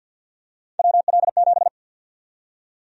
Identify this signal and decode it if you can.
Morse code, heard as a keyed tone, about 53 wpm, 700 Hz. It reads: WL7
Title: USA Call Sign Prefixes - 50wpm